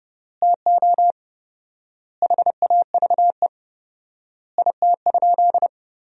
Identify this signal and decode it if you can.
Morse code, heard as a keyed tone, about 30 wpm, 700 Hz. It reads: TO HAVE IT?
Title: 4x Word-Spacing - Sentences from Top 100 Words - 30wpm